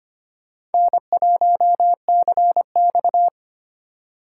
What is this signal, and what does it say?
Morse code, heard as a keyed tone, about 32 wpm, 700 Hz. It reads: N1CX